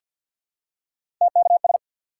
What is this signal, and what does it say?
Morse code, heard as a keyed tone, about 47 wpm, 700 Hz. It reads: TKS
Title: Top 180 QSO Elements 50wpm